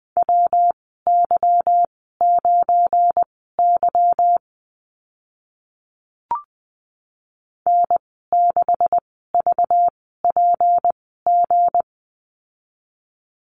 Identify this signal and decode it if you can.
Morse code, heard as a keyed tone, about 20 wpm, spaced as for 14 wpm, 700 Hz. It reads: WY9Y N6VPG
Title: 231 Lesson 37g - Letter x - Call Signs 20wpm